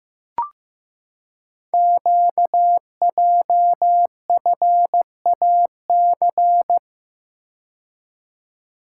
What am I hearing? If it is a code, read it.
Morse code, heard as a keyed tone, about 15 wpm, 700 Hz. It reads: QJFAC
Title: Code Groups - 5 Letters - 15wpm